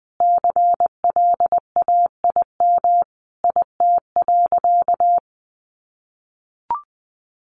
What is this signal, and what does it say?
Morse code, heard as a keyed tone, about 20 wpm, 700 Hz. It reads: CLAIM IT.
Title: Sentences from Top 1500 Words 20wpm